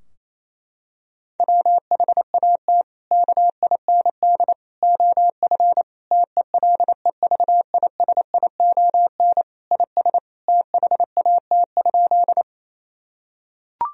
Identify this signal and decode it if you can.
Morse code, heard as a keyed tone, about 28 wpm, 700 Hz. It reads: WHAT KIND OF TELEVISION IS THAT?
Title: Sentences from Top 1000 Words - Review Part 4 - 28wpm